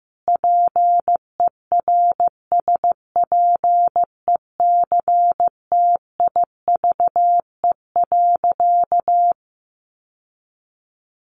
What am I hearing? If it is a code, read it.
Morse code, heard as a keyed tone, about 15 wpm, 700 Hz. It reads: PERSPECTIVE.
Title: Sentences from Top 1200 Words 15wpm